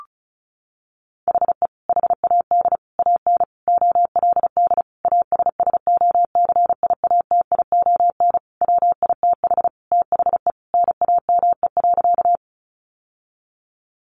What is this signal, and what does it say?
Morse code, heard as a keyed tone, about 34 wpm, 700 Hz. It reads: HE HAD AN OLD ASSOCIATION WITH THE NAME.
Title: Sentences from Top 1100 Words 35wpm